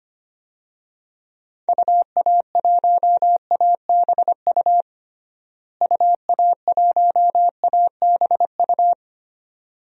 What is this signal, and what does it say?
Morse code, heard as a keyed tone, about 25 wpm, 700 Hz. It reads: UA1ABU UA1ABU
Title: DXCC-100 Call Signs - 6 Characters - Repeated 2x - 25wpm